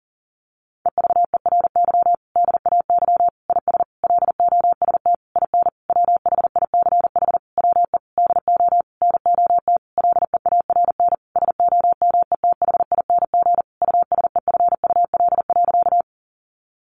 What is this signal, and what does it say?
Morse code, heard as a keyed tone, about 40 wpm, 700 Hz. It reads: EVERY DAY IS LOST IN WHICH WE DO NOT LEARN SOMETHING USEFUL.
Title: Sentences from Top 1400 Words 40wpm